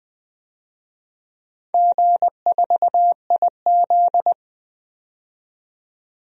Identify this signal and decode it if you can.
Morse code, heard as a keyed tone, about 20 wpm, 700 Hz. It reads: G4IZ
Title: DXCC-100 Call Signs - 4 Characters - 20wpm